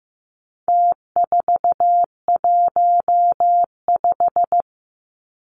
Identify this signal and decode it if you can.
Morse code, heard as a keyed tone, about 15 wpm, 700 Hz. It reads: T415
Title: Numbers - Contest Style - 4-Digit 15wpm